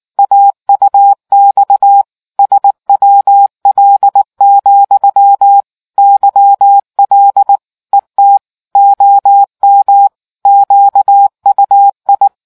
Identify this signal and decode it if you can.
Morse code, heard as a keyed tone, about 19 wpm, 800 Hz. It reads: AUX SWL, YL ET OM QUI